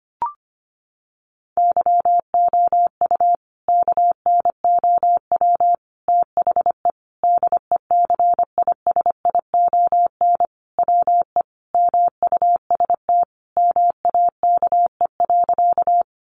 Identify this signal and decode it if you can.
Morse code, heard as a keyed tone, about 25 wpm, 700 Hz. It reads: YOU KNOW THE DECISION WE MUST MAKE.
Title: Sentences from Top 1000 Words 25wpm